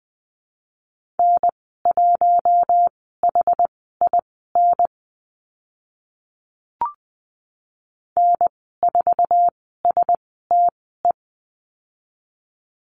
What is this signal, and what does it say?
Morse code, heard as a keyed tone, about 20 wpm, spaced as for 14 wpm, 700 Hz. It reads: N1HIN N4STE
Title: 062 Lesson 11g - Letter h - Call Signs 20wpm